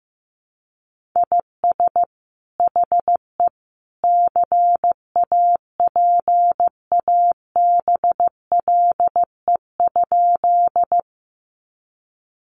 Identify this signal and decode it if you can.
Morse code, heard as a keyed tone, about 15 wpm, 700 Hz. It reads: IS HE CAPABLE?
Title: Sentences from Top 1200 Words 15wpm